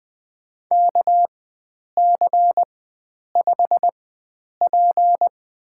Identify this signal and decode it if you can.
Morse code, heard as a keyed tone, about 20 wpm, spaced as for 9 wpm, 700 Hz. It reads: KC5P
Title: USA Call Signs - 4x Character Spacing - 20wpm